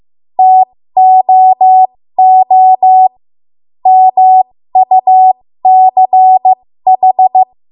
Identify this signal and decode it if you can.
Morse code, heard as a keyed tone, about 15 wpm, spaced as for 13 wpm, 750 Hz. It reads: TOO MUCH
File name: Morse / 13WPM